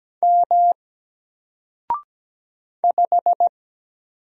Morse code, heard as a keyed tone, about 17 wpm, 700 Hz. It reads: M 5
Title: Single Letter-Number 17wpm